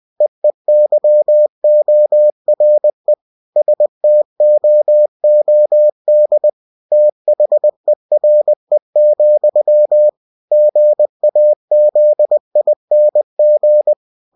Morse code, heard as a keyed tone, about 20 wpm, 600 Hz. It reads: EEYORE STOOD THERE, GAZING
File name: cw-books / winnie-the-pooh / winnie-20wpm-page-0026